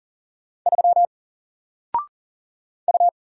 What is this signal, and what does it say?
Morse code, heard as a keyed tone, about 40 wpm, 700 Hz. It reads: 3 U
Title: Single Character (Letter:Number:Punctuation:Prosign) Alt 40wpm